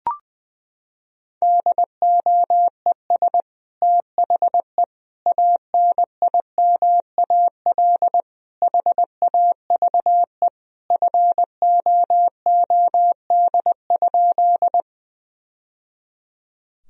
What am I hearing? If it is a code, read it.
Morse code, heard as a keyed tone, about 20 wpm, 700 Hz. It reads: DOES THE ANIMAL HAVE FOOD?
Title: Sentences from Top 1000 Words - Review Part 1 - 20wpm